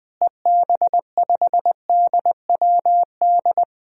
Morse code, heard as a keyed tone, about 20 wpm, 700 Hz. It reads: EB5DWD
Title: DXCC-100 Call Signs - 6 Characters - Repeated 3x - 20wpm